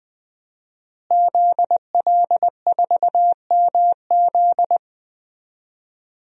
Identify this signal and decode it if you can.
Morse code, heard as a keyed tone, about 20 wpm, 700 Hz. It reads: ZL4MZ
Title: DXCC-100 Call Signs - 5 Characters - Repeated 2x - 20wpm